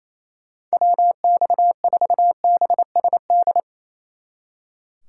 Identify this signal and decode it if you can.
Morse code, heard as a keyed tone, about 36 wpm, 700 Hz. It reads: WX4BSD